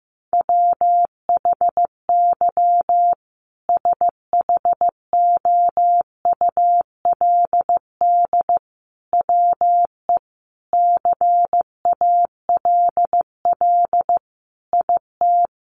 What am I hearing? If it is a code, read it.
Morse code, heard as a keyed tone, about 15 wpm, 700 Hz. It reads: WHY SHOULD WE CALL IT
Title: Sentences from Top 1100 Words 15wpm